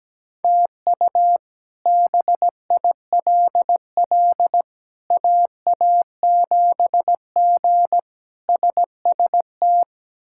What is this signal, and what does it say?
Morse code, heard as a keyed tone, about 17 wpm, 700 Hz. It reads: TU BILL AA7G SST
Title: K1USN SST QSO Practice - 17wpm